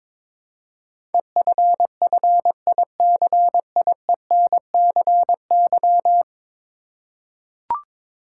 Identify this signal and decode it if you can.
Morse code, heard as a keyed tone, about 22 wpm, 700 Hz. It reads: EFFICIENCY
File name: practice / 22wpm / 10 Letter Words - First 3 Letters Given - 22wpm